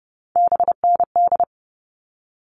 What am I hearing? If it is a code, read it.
Morse code, heard as a keyed tone, about 30 wpm, 700 Hz. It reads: BND
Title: All Cut Numbers - 3-Digit 30wpm